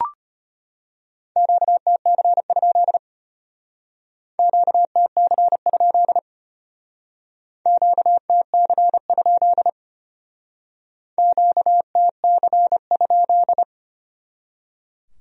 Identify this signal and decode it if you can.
Morse code, heard as a keyed tone, about 32 wpm, 700 Hz. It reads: QTC? QTC? QTC? QTC?